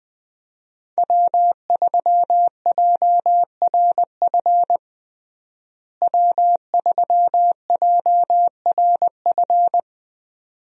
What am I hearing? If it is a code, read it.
Morse code, heard as a keyed tone, about 20 wpm, 700 Hz. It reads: W3JRF W3JRF
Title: USA Call Signs - 5 Characters - Repeated 2x - 20wpm